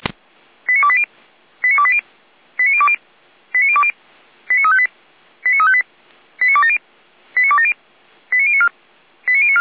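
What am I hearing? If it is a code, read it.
Five-tone selective call (ZVEI), heard as quick runs of five tones, tones of about 70 ms. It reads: 89280 89280 89020 89020 89358 89358 89270 89270 89094 89094
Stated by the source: ZVEI five-tone selective call 89230 received by SDR, from the Feuerwehr Orlamünde (fire brigade) audio server.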